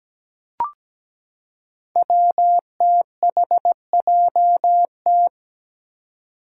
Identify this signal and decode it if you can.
Morse code, heard as a keyed tone, about 17 wpm, 700 Hz. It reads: WTHJT
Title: Five Challenging Letter-Number Combinations 17wpm